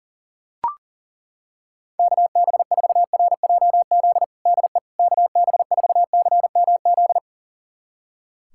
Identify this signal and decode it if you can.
Morse code, heard as a keyed tone, about 37 wpm, 700 Hz. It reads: KB4RJZ DE KB4CMZ
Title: USA Callsign DE Callsign - 6 Characters - 40wpm